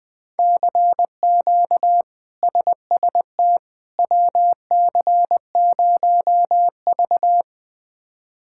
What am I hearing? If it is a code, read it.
Morse code, heard as a keyed tone, about 20 wpm, 700 Hz. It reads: CQ SST WC0V
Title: K1USN SST QSO Practice - 20wpm